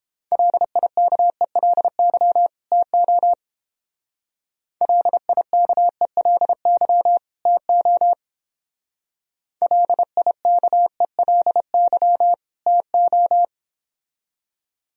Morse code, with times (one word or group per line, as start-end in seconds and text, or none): 0.3-2.5 LIKELY
2.7-3.3 TO
4.8-7.2 LIKELY
7.5-8.1 TO
9.6-12.3 LIKELY
12.7-13.4 TO